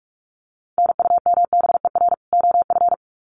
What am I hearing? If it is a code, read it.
Morse code, heard as a keyed tone, about 44 wpm, 700 Hz. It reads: NUMBER OF
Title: Sets of 2 Words 45wpm